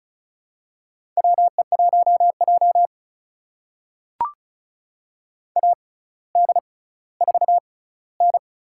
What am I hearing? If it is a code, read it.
Morse code, heard as a keyed tone, about 34 wpm, spaced as for 18 wpm, 700 Hz. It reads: WE1J AD4N
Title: USA Call Signs - 4 Characters - 6x Character Spacing - 35wpm